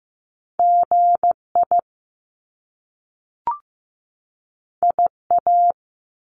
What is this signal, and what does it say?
Morse code, heard as a keyed tone, about 15 wpm, 700 Hz. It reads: GI IA
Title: Top Two Letter Combinations in Words 15wpm